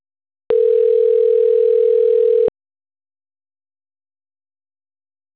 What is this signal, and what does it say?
Ringback tone (North America), 440 + 480 Hz.